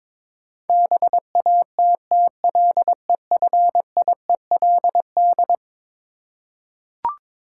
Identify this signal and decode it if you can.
Morse code, heard as a keyed tone, about 21 wpm, 700 Hz. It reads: BATTLEFIELD
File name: practice / 22wpm / Eleven Letter Words - 22wpm